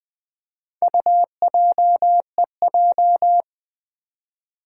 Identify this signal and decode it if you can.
Morse code, heard as a keyed tone, about 21 wpm, 700 Hz. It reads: UJEJ
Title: Four Challenging Letter-Number Combinations 20wpm